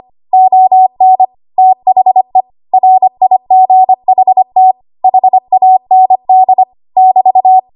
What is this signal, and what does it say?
Morse code, heard as a keyed tone, about 25 wpm, 750 Hz. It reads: ON THE RIGHT HAND <BT>
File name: Archive / 25 WPM / 25WPM